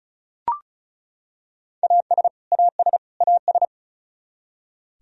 Morse code, heard as a keyed tone, about 36 wpm, 700 Hz. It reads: AS AS AS
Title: Top 100 Words Repeated 35wpm